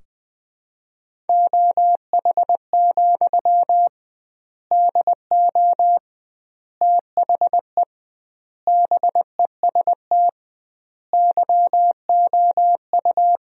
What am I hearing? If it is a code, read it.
Morse code, heard as a keyed tone, about 20 wpm, 700 Hz. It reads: OH, DO THE BEST YOU
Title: 2x Word-Spacing - Sentences from Top 500 Words - 20wpm